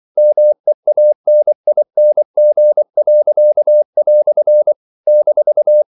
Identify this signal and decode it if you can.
Morse code, heard as a keyed tone, about 24 wpm, 600 Hz. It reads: MEANING." -
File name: CW / Practice / EnglishQuotations0324